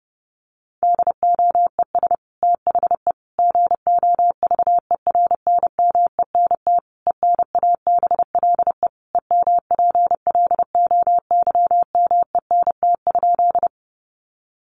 Morse code, heard as a keyed tone, about 30 wpm, 700 Hz. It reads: DOES THE GOVERNMENT ENABLE EMPLOYMENT?
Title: Sentences from Top 1500 Words 30wpm